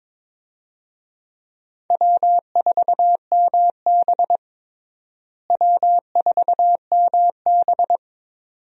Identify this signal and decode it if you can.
Morse code, heard as a keyed tone, about 22 wpm, 700 Hz. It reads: W4MB W4MB
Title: USA Call Signs - 4 Characters - Repeated 2x - 22wpm